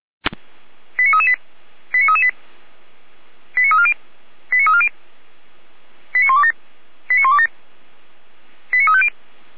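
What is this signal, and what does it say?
Five-tone selective call (ZVEI), heard as quick runs of five tones, tones of about 70 ms. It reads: 89338 89338 89340 89340 89127 89127 89350
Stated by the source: ZVEI five-tone call 89337 received by SDR, recorded by Feuerwehr Orlamünde fire brigade.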